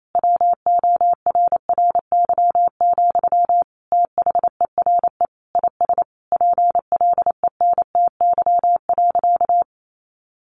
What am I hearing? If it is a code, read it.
Morse code, heard as a keyed tone, about 28 wpm, 700 Hz. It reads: WORRY, THERE IS PLENTY.
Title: Sentences from Top 1300 Words 28wpm